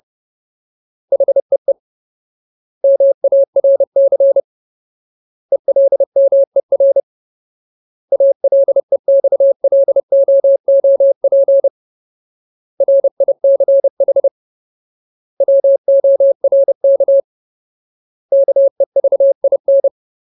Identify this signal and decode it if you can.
Morse code, heard as a keyed tone, about 30 wpm, 550 Hz. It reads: HEE MARC ELMER ALEXLOOP RICH WORK KEVIN
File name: Ultimate Word List / 30WPM Ultimate